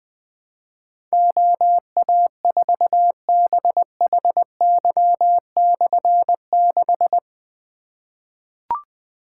Morse code, heard as a keyed tone, about 20 wpm, 700 Hz. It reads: OA4BHY/6
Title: Difficult Call Signs 20wpm